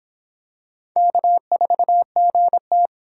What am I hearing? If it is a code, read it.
Morse code, heard as a keyed tone, about 24 wpm, 700 Hz. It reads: K4GT